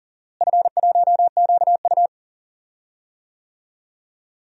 Morse code, heard as a keyed tone, about 40 wpm, 700 Hz. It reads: F1QU
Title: DXCC-100 Call Signs - 4 Characters - Repeated 2x - 40wpm